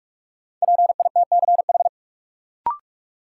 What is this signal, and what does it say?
Morse code, heard as a keyed tone, about 46 wpm, 700 Hz. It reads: PITCH
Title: Sentences from Top 1300 Words - New Word Drill - 45wpm